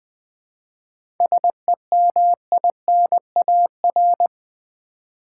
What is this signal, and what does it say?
Morse code, heard as a keyed tone, about 20 wpm, 700 Hz. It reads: SEMINAR
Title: 7 Letter Words - First 3 Letters Given - 20wpm